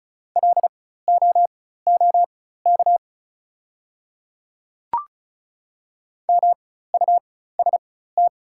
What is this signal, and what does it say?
Morse code, heard as a keyed tone, about 36 wpm, spaced as for 16 wpm, 690 Hz. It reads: LOOK MUST
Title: 4x Character Spacing - Top 300 Words - 35wpm